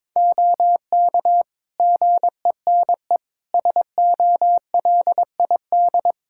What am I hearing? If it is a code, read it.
Morse code, heard as a keyed tone, about 22 wpm, 700 Hz. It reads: OK GENE SOLID
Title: Practice QSO - Exchange 3 - More Info - 22wpm